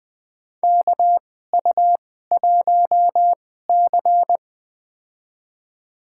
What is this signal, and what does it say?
Morse code, heard as a keyed tone, about 20 wpm, spaced as for 15 wpm, 700 Hz. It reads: KU1C
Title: USA Call Signs - 4 Characters - 2x Character Spacing - 20wpm